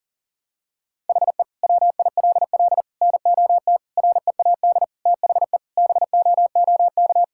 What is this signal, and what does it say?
Morse code, heard as a keyed tone, about 40 wpm, 700 Hz. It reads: HE WILL NOT READ THE BOOK
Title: Sentences from Top 1600 Words 40wpm